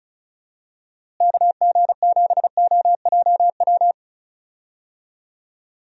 Morse code, heard as a keyed tone, about 36 wpm, 700 Hz. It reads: KG7OJW
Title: USA Call Signs - 6 Characters - Repeated 2x - 35wpm